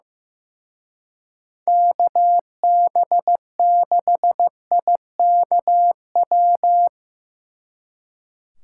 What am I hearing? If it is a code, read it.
Morse code, heard as a keyed tone, about 16 wpm, 700 Hz. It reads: KB6IKW
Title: Speed Racing - USA 2x3 Call Signs 15wpm